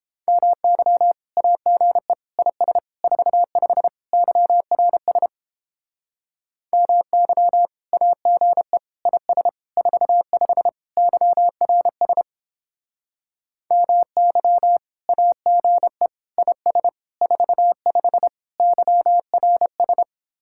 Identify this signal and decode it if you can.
Morse code, heard as a keyed tone, about 31 wpm, 700 Hz. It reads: MY AGE IS 45 YRS MY AGE IS 45 YRS MY AGE IS 45 YRS